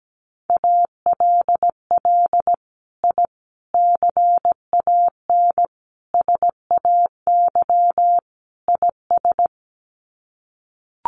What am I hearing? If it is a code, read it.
Morse code, heard as a keyed tone, about 17 wpm, 700 Hz. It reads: ALL I CAN SAY IS
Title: Sets of 5 Words Encore2 17wpm